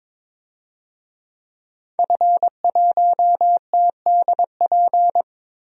Morse code, heard as a keyed tone, about 22 wpm, 700 Hz. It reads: F1TDP
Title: DXCC-100 Call Signs - 5 Characters - Repeated 2x - 22wpm